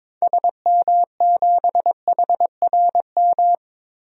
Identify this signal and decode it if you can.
Morse code, heard as a keyed tone, about 22 wpm, 700 Hz. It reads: SM7HRM
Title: DXCC-100 Call Signs - 6 Characters - 22wpm